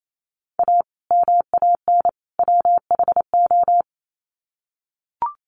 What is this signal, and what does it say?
Morse code, heard as a keyed tone, about 28 wpm, 700 Hz. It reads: A MAN WHO
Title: Sets of 3 Words 28wpm